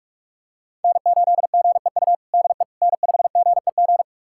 Morse code, heard as a keyed tone, about 46 wpm, 700 Hz. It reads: N8GEU DE N5GEG
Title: USA Callsign DE Callsign - 5 Characters - 45wpm